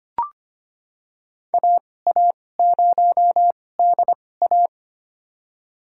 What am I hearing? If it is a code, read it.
Morse code, heard as a keyed tone, about 25 wpm, spaced as for 18 wpm, 700 Hz. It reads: AA0DA
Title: USA Call Signs - 5 Characters - 2x Character Spacing - 25wpm